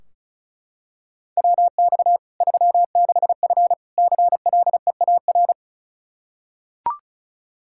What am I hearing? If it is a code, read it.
Morse code, heard as a keyed tone, about 34 wpm, 700 Hz. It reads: WX 36F CLEAR
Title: Practice QSO - Exchange 2 - Info - 35wpm